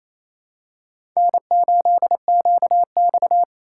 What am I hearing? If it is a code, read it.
Morse code, heard as a keyed tone, about 28 wpm, 700 Hz. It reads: N8QX
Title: USA Call Signs - 4 Characters - 4x Character Spacing - 28wpm